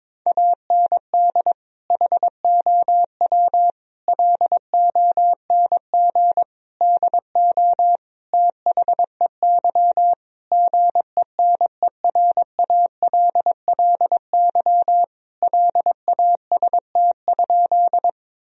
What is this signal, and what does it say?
Morse code, heard as a keyed tone, about 22 wpm, 700 Hz. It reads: AND HOW LONG DO THEY GENERALLY LAST?
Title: Sentences from Top 1000 Words - Review Part 4 - 22wpm